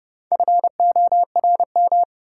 Morse code, heard as a keyed tone, about 30 wpm, 700 Hz. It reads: FORM
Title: Sentences from Top 1000 Words - Review Part 1 - New Word Drill - 30wpm